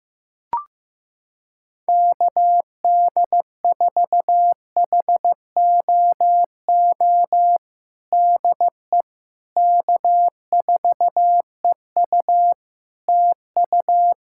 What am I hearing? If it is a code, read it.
Morse code, heard as a keyed tone, about 15 wpm, 700 Hz. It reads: KD4HOO DE K4EU TU
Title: Practice QSO - Exchange 4 - Ending - 15wpm